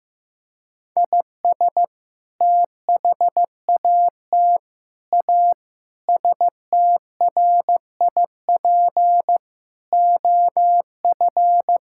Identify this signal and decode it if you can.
Morse code, heard as a keyed tone, about 15 wpm, 700 Hz. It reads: IS THAT A STRIP OF
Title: Sentences from Top 1600 Words 15wpm